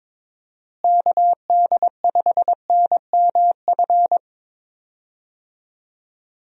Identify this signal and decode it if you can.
Morse code, heard as a keyed tone, about 22 wpm, 700 Hz. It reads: KD5NMF
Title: USA Call Signs - 6 Characters - Repeated 3x - 22wpm